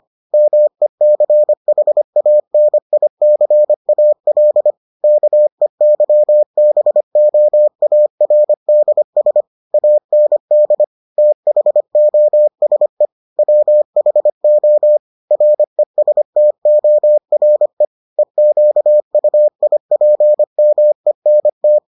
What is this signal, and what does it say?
Morse code, heard as a keyed tone, about 25 wpm, 600 Hz. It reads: MECHANICAL KEYBOARDS AND THOSE WHO RESTORE EQUIPMENT